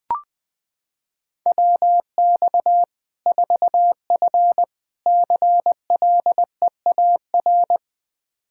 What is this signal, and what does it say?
Morse code, heard as a keyed tone, about 20 wpm, 700 Hz. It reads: WX 4F CLEAR
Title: Practice QSO - Exchange 2 - Info - 20wpm